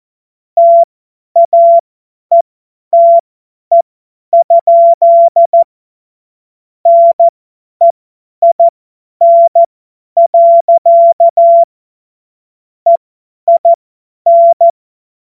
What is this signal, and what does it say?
Morse code, heard as a keyed tone, about 14 wpm, spaced as for 10 wpm, 680 Hz. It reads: TAETE? NEIN. EIN
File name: Morsekurs / Witze / Witz 13. Januar 2026 50 F70